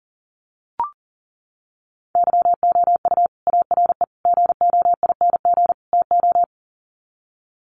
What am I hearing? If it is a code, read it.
Morse code, heard as a keyed tone, about 40 wpm, 700 Hz. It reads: YOU ARE GOING TO
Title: Sets of 4 Words 40wpm